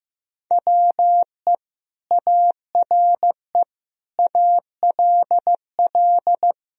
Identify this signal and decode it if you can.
Morse code, heard as a keyed tone, about 15 wpm, 700 Hz. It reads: WE ARE ALL
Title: Sentences from Top 1000 Words - Review Part 2 - 15wpm